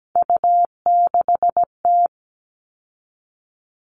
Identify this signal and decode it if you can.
Morse code, heard as a keyed tone, about 17 wpm, 700 Hz. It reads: U6T
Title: All Cut Numbers - 3-Digit 17wpm